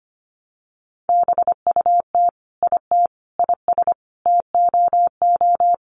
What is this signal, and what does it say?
Morse code, heard as a keyed tone, about 25 wpm, 700 Hz. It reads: BUT IT IS TOO
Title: Sentences from Top 1100 Words 25wpm